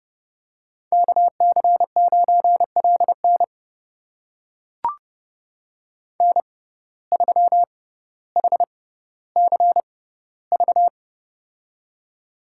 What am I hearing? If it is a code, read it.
Morse code, heard as a keyed tone, about 30 wpm, spaced as for 15 wpm, 700 Hz. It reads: KC9LN N3HCV
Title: USA Call Signs - 5 Characters - 6x Character Spacing - 30wpm